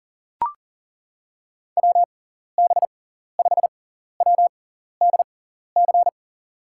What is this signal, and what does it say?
Morse code, heard as a keyed tone, about 39 wpm, spaced as for 14 wpm, 700 Hz. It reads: WB5WDC
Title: USA Call Signs - 6 Characters - 6x Character Spacing - 40wpm